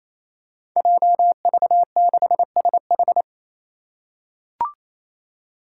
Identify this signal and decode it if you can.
Morse code, heard as a keyed tone, about 28 wpm, 700 Hz. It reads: JV6SH
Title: Five Challenging Letter-Number Combinations 28wpm